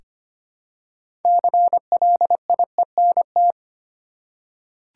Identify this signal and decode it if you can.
Morse code, heard as a keyed tone, about 25 wpm, 700 Hz. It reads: CLIENT
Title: Six Letter Words - 25wpm